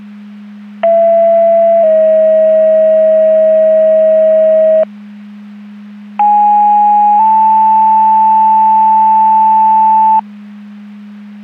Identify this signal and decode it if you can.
Two-tone page, heard as a one-second tone, then a three-second tone, about 670 then 640 Hz, about 830 then 880 Hz. It is heard from 0.8 to 10.2 s.
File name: Portage County 154.3850 Fire EMS Paging